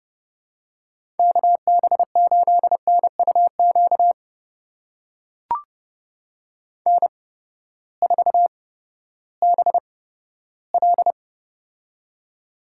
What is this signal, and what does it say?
Morse code, heard as a keyed tone, about 30 wpm, spaced as for 15 wpm, 700 Hz. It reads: KB8NUQ N4BL
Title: USA Call Signs - 8x Character Spacing - 30wpm